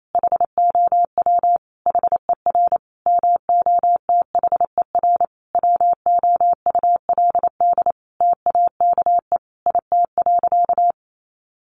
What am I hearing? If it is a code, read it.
Morse code, heard as a keyed tone, about 28 wpm, 700 Hz. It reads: HOW HER MOTHER WOULD TAKE IT.
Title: Sentences from Top 700 Words 28wpm